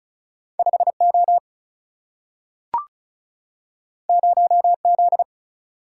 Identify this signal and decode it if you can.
Morse code, heard as a keyed tone, about 36 wpm, 700 Hz. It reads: 5 O 0 Z
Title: Single-Letter Single-Number 35wpm